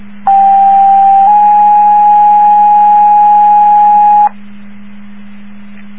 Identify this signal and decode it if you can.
Two-tone page, heard as a one-second tone, then a three-second tone, about 790 then 830 Hz. It is heard from 0.3 to 4.3 s.